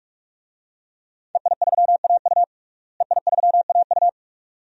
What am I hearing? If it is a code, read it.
Morse code, heard as a keyed tone, about 48 wpm, 700 Hz. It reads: EI3AU EI3AU
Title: DXCC-100 Call Signs - 5 Characters - Repeated 2x - 45wpm